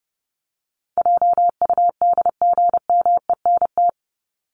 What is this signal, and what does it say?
Morse code, heard as a keyed tone, about 30 wpm, 700 Hz. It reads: JUDGMENT
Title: Top 1751-2000 Words 30wpm